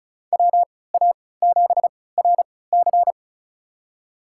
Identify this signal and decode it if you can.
Morse code, heard as a keyed tone, about 34 wpm, spaced as for 20 wpm, 700 Hz. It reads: WA7RC
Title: USA Call Signs - 5 Characters - 3x Character Spacing - 35wpm